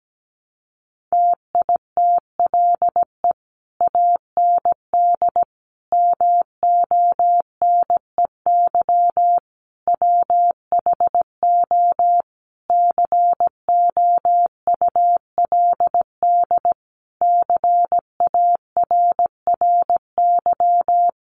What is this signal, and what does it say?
Morse code, heard as a keyed tone, about 17 wpm, 700 Hz. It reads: TITLE AND MONEY WHO COULD CARRY
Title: Sentences from Top 1200 Words 17wpm